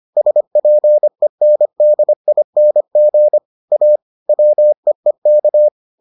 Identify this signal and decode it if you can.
Morse code, heard as a keyed tone, about 25 wpm, 600 Hz. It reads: SPENDING A WEEK